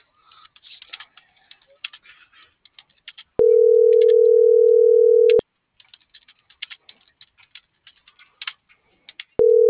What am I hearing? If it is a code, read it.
Ringback tone (North America), 440 + 480 Hz.
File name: CENTRAL MULTIMODAL / out josimar.cruz - .85513